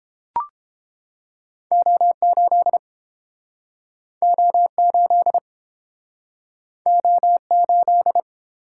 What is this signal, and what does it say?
Morse code, heard as a keyed tone, about 32 wpm, 700 Hz. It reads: O8 O8 O8